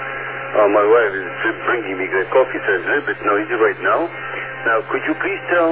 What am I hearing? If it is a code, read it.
Emergency Alert System (SAME) tones, heard as harsh buzzing bursts, with speech. It is heard from 1.2 to 4.8 s.